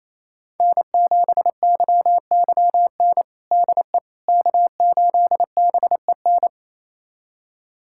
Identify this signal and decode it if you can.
Morse code, heard as a keyed tone, about 28 wpm, 700 Hz. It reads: N7YYN DE K8BEN
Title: USA Callsign DE Callsign - 5 Characters - 28wpm